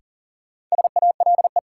Morse code, heard as a keyed tone, about 40 wpm, 700 Hz. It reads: SALE
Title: Sentences from Top 1400 Words - New Word Drill Alt - 40wpm